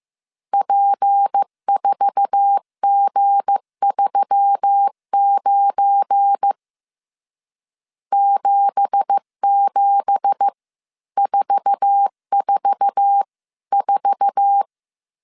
Morse code, heard as a keyed tone, about 15 wpm, spaced as for 13 wpm, 800 Hz. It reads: P4G39 77444
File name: CW / DET / 1 9 1500C S C 14wpm 3245